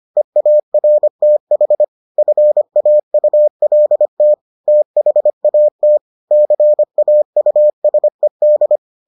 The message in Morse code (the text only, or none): EARTH FAULT THAT CAUSED